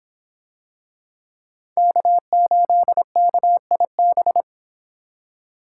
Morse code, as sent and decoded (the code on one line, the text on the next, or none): -.- ---.. -.- .. -...
K8KIB